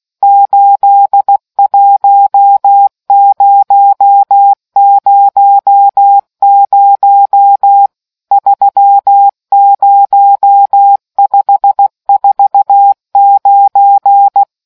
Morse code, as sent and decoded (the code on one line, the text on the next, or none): ---.. .---- ----- ----- ----- / ...-- ----- ..... ....- ----.
81000 30549